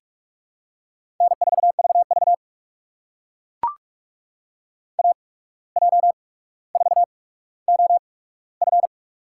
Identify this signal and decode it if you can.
Morse code, heard as a keyed tone, about 44 wpm, spaced as for 18 wpm, 700 Hz. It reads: N4VV AJ4OF
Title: USA Call Signs - 8x Character Spacing - 45wpm